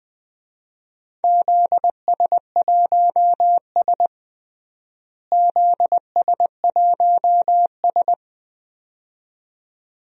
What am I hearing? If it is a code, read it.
Morse code, heard as a keyed tone, about 21 wpm, 700 Hz. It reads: ZS1S ZS1S
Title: DXCC-100 Call Signs - 4 Characters - Repeated 2x - 20wpm